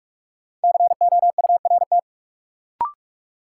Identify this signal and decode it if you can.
Morse code, heard as a keyed tone, about 42 wpm, 700 Hz. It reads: COURT
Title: Sentences from Top 1600 Words - New Word Drill - 45wpm